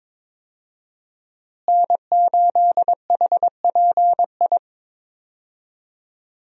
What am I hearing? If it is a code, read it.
Morse code, heard as a keyed tone, about 22 wpm, 700 Hz. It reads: N8HPI
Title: USA Call Signs - 5 Characters - Repeated 2x - 22wpm